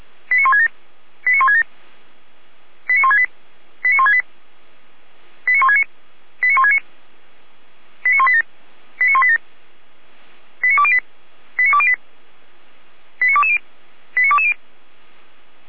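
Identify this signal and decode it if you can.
Five-tone selective call (ZVEI), heard as quick runs of five tones, tones of about 80 ms. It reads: 89167 89167 89168 89168 89169 89169 89187 89187 89208 89208 89220 89220
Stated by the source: ZVEI five-tone call received by SDR.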